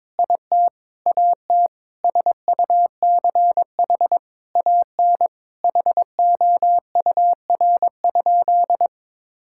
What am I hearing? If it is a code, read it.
Morse code, heard as a keyed tone, about 22 wpm, 700 Hz. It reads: IT AT SUCH AN HOUR?
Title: Sentences from Top 1000 Words - Review Part 5 - 22wpm